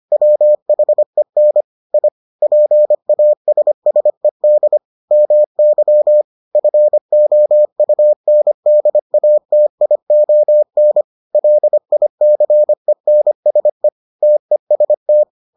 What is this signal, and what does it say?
Morse code, heard as a keyed tone, about 25 wpm, 600 Hz. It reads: WHEN I PASSED MY FOUNDATION LICENSE TEST